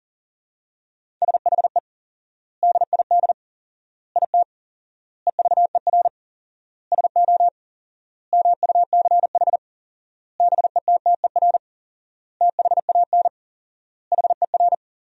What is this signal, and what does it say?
Morse code, heard as a keyed tone, about 40 wpm, 700 Hz. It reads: SHE DID IT EVER SO MUCH BETTER THAN HER
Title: 4x Word-Spacing - Sentences from Top 400 Words - 40wpm